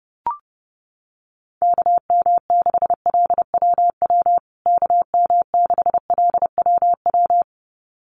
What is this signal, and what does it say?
Morse code, heard as a keyed tone, about 30 wpm, 700 Hz. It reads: KM6LWW KM6LWW
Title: USA Call Signs Repeated 2x 30wpm